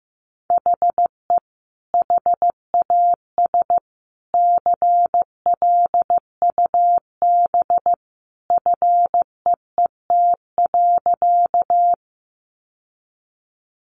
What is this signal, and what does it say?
Morse code, heard as a keyed tone, about 15 wpm, 700 Hz. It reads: HE HAS CLUB FEET.
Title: Sentences from Top 1200 Words 15wpm